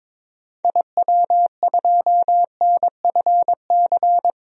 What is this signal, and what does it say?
Morse code, heard as a keyed tone, about 21 wpm, 700 Hz. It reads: IW2NFC
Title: DXCC-100 Call Signs - 6 Characters - 22wpm